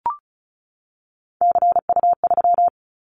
Morse code, heard as a keyed tone, about 36 wpm, 700 Hz. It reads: CU3
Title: Callsign Prefixes for Top 100 DXCC - 35wpm